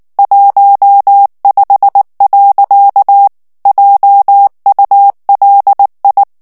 Morse code, heard as a keyed tone, about 19 wpm, 800 Hz. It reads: 15. JULI
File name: morse mp3 / ALLEMAND texte clair 2 vitesse 18 mots minute